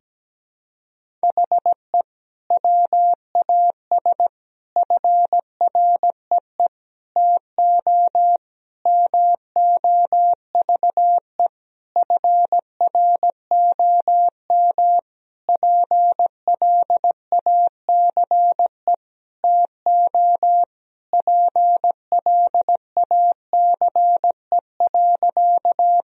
Morse code, heard as a keyed tone, about 17 wpm, 700 Hz. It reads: HE WAS FREE TO MOVE FROM PLACE TO PLACE.
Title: Sentences from Top 1000 Words - Review Part 1 - 17wpm